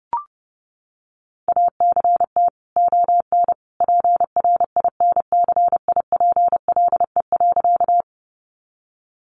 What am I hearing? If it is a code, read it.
Morse code, heard as a keyed tone, about 30 wpm, 700 Hz. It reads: ACT ON PRINCIPLE.
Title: Sentences from Top 1400 Words 30wpm